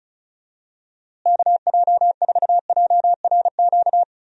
Morse code, heard as a keyed tone, about 34 wpm, 700 Hz. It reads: KJ4JRQ
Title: USA Call Signs - 6 Characters - Repeated 2x - 35wpm